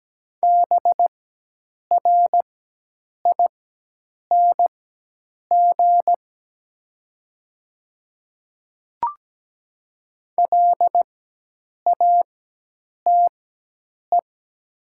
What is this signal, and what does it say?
Morse code, heard as a keyed tone, about 17 wpm, spaced as for 8 wpm, 700 Hz. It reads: BRING LATE
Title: 4x Character Spacing - Top 500 Words - 17wpm